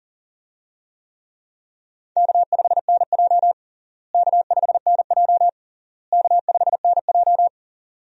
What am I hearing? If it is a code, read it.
Morse code, heard as a keyed tone, about 40 wpm, 700 Hz. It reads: K5NJ K5NJ K5NJ
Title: USA Call Signs - 4 Characters - Repeated 3x - 40wpm